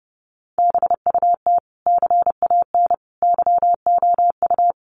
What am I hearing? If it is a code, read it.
Morse code, heard as a keyed tone, about 30 wpm, 700 Hz. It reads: BUT CAN YOU
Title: Sentences from Top 1400 Words 30wpm